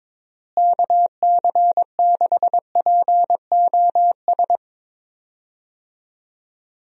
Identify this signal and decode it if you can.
Morse code, heard as a keyed tone, about 22 wpm, 700 Hz. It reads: KC6POS
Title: USA Call Signs - 6 Characters - Repeated 2x - 22wpm